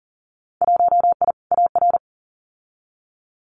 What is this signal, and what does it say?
Morse code, heard as a keyed tone, about 38 wpm, 700 Hz. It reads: 1I AR
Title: Winter Field Day Exchanges Encore 40wpm